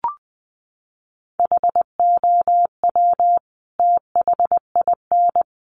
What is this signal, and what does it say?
Morse code, heard as a keyed tone, about 19 wpm, 700 Hz. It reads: HOW THIN
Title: Sentences from Top 1200 Words 20wpm